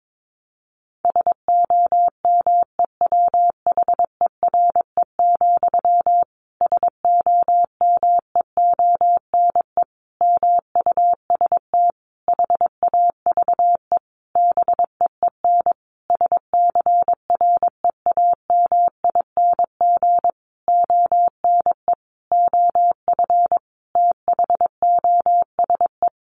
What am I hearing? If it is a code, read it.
Morse code, heard as a keyed tone, about 22 wpm, 700 Hz. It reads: SOMEWHERE, SOMEONE MUST HAVE BEEN SCREAMING ONE OF THOSE
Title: A Sound Of Thunder 22wpm